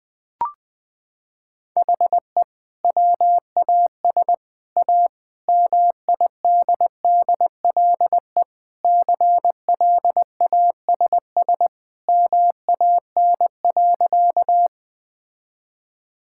Morse code, heard as a keyed tone, about 20 wpm, 700 Hz. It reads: HE WAS A MIDDLE CLASS MAN.
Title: Sentences from Top 1000 Words - Review Part 6 - 20wpm